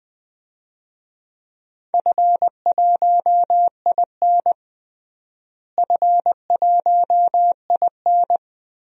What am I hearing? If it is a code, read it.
Morse code, heard as a keyed tone, about 20 wpm, 700 Hz. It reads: F1IN F1IN
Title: DXCC-100 Call Signs - 4 Characters - Repeated 3x - 20wpm